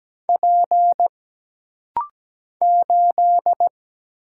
Morse code, heard as a keyed tone, about 17 wpm, 700 Hz. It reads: P 8
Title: Single Letter-Number - Rapid-Fire 17wpm